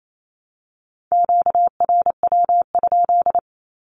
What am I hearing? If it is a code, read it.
Morse code, heard as a keyed tone, about 28 wpm, 700 Hz. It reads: QRW?
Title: Q-Codes Repeated 3x 28wpm